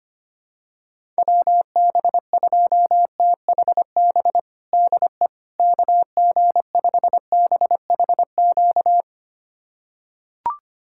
Morse code, heard as a keyed tone, about 25 wpm, 700 Hz. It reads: WB2THB DE KG5BHQ
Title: USA Callsign DE Callsign 25wpm